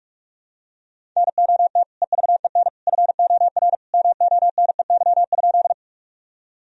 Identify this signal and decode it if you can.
Morse code, heard as a keyed tone, about 44 wpm, 700 Hz. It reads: NOT EVEN FOR MONEY?
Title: Sentences from Top 1000 Words - Review Part 1 - 45wpm